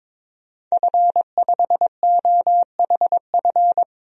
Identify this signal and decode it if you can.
Morse code, heard as a keyed tone, about 22 wpm, 700 Hz. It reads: F5OHF
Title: DXCC-100 Call Signs - 5 Characters - 22wpm